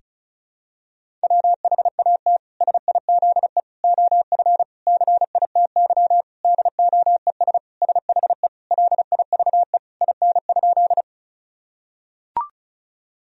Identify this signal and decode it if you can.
Morse code, heard as a keyed tone, about 36 wpm, 700 Hz. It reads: WHAT SIZE OF CITY DOES SHE LIVE IN?
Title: Sentences from Top 1000 Words - Review Part 2 - 35wpm